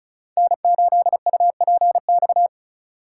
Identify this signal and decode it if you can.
Morse code, heard as a keyed tone, about 34 wpm, 700 Hz. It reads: N8UPX
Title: USA Call Signs - 5 Characters - 3x Character Spacing - 35wpm